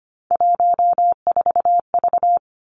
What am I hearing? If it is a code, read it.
Morse code, heard as a keyed tone, about 25 wpm, 700 Hz. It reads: 14V
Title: Three Challenging Letter-Number Combinations 25wpm